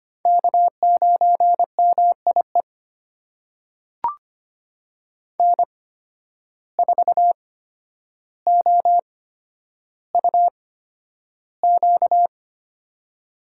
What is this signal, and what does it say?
Morse code, heard as a keyed tone, about 25 wpm, spaced as for 11 wpm, 700 Hz. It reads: K9MIE N4OUQ
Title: USA Call Signs - 5 Characters - 8x Character Spacing - 25wpm